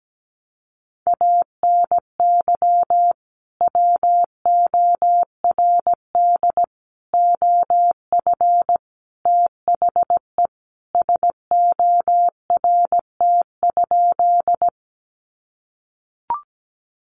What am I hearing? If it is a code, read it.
Morse code, heard as a keyed tone, about 17 wpm, 700 Hz. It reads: ANY WORD OF THE SORT?
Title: Sentences from Top 1300 Words 17wpm